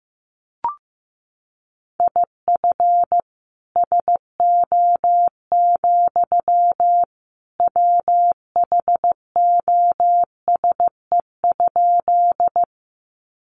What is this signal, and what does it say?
Morse code, heard as a keyed tone, about 15 wpm, 700 Hz. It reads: IF SO, WHOSE?
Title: Sentences from Top 1200 Words 15wpm